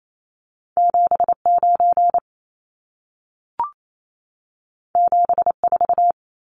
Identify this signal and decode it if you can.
Morse code, heard as a keyed tone, about 28 wpm, 700 Hz. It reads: 79 74
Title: Numbers - 2-Digit 28wpm